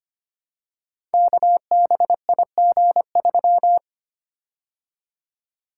Morse code, heard as a keyed tone, about 25 wpm, 700 Hz. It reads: KBIG3
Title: Code Groups - 5 Letter:Numbers - 25wpm